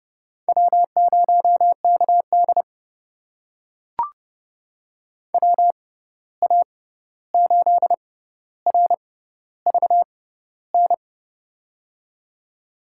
Morse code, heard as a keyed tone, about 30 wpm, spaced as for 14 wpm, 700 Hz. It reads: W0KD WA8RVN
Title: USA Call Signs - 6x Character Spacing - 30wpm